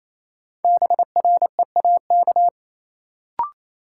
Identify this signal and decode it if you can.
Morse code, heard as a keyed tone, about 28 wpm, 700 Hz. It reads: BREAK
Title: Sets of 4 Words - New Word Drill - 28wpm